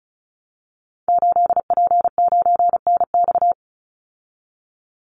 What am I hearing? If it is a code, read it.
Morse code, heard as a keyed tone, about 34 wpm, 700 Hz. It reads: 8P9NX
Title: Call Signs Repeated 2x - 35wpm